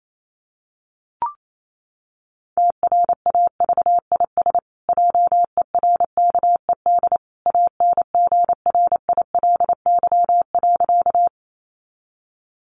Morse code, heard as a keyed tone, about 28 wpm, 700 Hz. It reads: TRAVIS JERKED ANGRILY.
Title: A Sound Of Thunder 28wpm